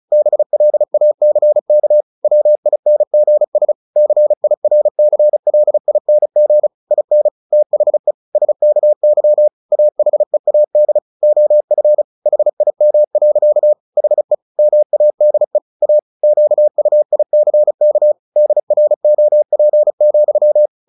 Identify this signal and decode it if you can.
Morse code, heard as a keyed tone, about 34 wpm, 600 Hz. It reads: BLACK WINGS CIRCLING IN THE SKY AHEAD OF HIM. HE MADE A QUICK DROP,
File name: the old man and the sea / the old man and the sea 1 35wpm